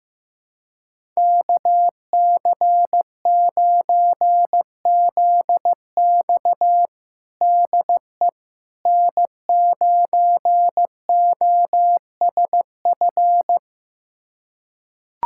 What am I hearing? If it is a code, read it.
Morse code, heard as a keyed tone, about 15 wpm, 700 Hz. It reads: KC9ZX DE N9OSF
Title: USA Callsign DE Callsign - 5 Characters - 15wpm